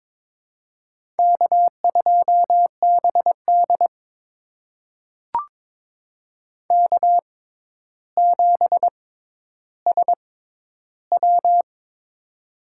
Morse code, heard as a keyed tone, about 22 wpm, spaced as for 11 wpm, 700 Hz. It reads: K2BD K7SW
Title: USA Call Signs - 6x Character Spacing - 22wpm